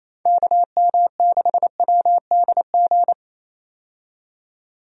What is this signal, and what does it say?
Morse code, heard as a keyed tone, about 27 wpm, 700 Hz. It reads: KM6WDG
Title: USA Call Signs - 6 Characters - 28wpm